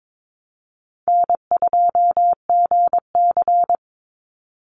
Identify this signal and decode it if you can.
Morse code, heard as a keyed tone, about 22 wpm, 700 Hz. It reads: N2GC
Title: Easy Call Signs - 22wpm